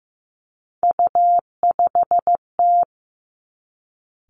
Morse code, heard as a keyed tone, about 15 wpm, 700 Hz. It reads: U5T
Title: Three Challenging Letter-Number Combinations 15wpm